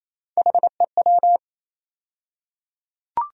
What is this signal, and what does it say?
Morse code, heard as a keyed tone, about 28 wpm, 700 Hz. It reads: HEW
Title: USA Call Sign Postfixes - 3 Letters 28wpm